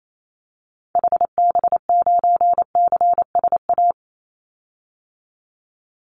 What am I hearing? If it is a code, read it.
Morse code, heard as a keyed tone, about 28 wpm, 700 Hz. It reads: HB9CSA
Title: Call Signs - 28wpm